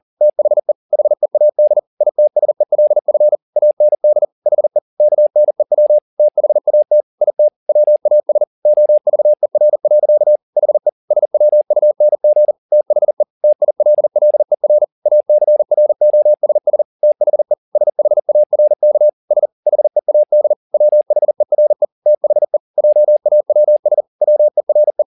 Morse code, heard as a keyed tone, about 40 wpm, 600 Hz. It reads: THE HEAD ITSELF AND HE KNEW THAT IT WAS OVER. HE SWUNG THE TILLER ACROSS THE SHARK S HEAD WHERE THE JAWS WERE
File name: the old man and the sea 2 40wpm